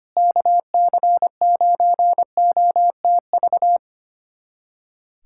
Morse code, heard as a keyed tone, about 25 wpm, 700 Hz. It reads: KC9OTV
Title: USA Callsign DE Callsign 25wpm